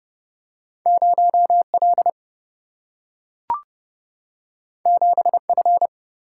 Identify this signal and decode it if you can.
Morse code, heard as a keyed tone, about 30 wpm, 700 Hz. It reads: 0 L 7 F
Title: Single-Letter Single-Number 30wpm